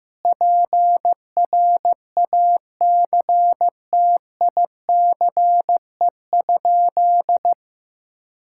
Morse code, heard as a keyed tone, about 15 wpm, 700 Hz. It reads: PRACTICE?
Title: Sentences from Top 1000 Words - Review Part 5 - 15wpm